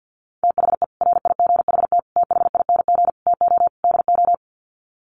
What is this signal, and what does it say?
Morse code, heard as a keyed tone, about 50 wpm, 700 Hz. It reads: THE RIGHT THING TO DO
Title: Sets of 5 Words 50wpm